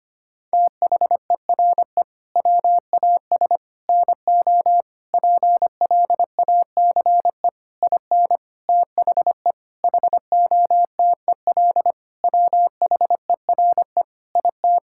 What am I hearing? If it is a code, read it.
Morse code, heard as a keyed tone, about 25 wpm, 700 Hz. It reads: THERE WAS NO PLACE IN THE HOTEL WHERE IT
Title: Sentences from Top 1600 Words 25wpm